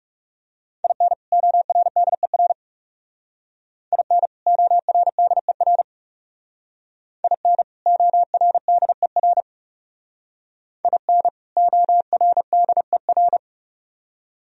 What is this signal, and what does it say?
Morse code, heard as a keyed tone, about 38 wpm, 700 Hz. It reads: IN ORDER IN ORDER IN ORDER IN ORDER